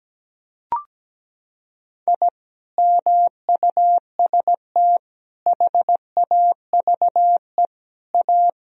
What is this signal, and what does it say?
Morse code, heard as a keyed tone, about 17 wpm, 700 Hz. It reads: I MUST HAVE A
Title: Sentences from Top 1000 Words - Review Part 4 - 17wpm